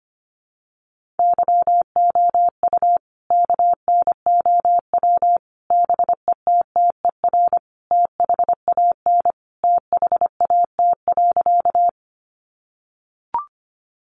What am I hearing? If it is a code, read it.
Morse code, heard as a keyed tone, about 25 wpm, 700 Hz. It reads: YOU KNOW BETTER THAN THAT.
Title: Sentences from Top 400 Words 25wpm